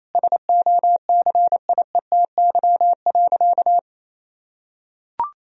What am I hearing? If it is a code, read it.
Morse code, heard as a keyed tone, about 28 wpm, 700 Hz. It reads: SOCIETY.
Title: Sentences from Top 1000 Words - Review Part 4 - 28wpm